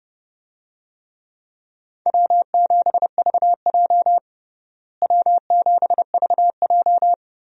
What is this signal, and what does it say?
Morse code, heard as a keyed tone, about 29 wpm, 700 Hz. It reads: W7VJ W7VJ
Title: USA Call Signs - 4 Characters - Repeated 2x - 30wpm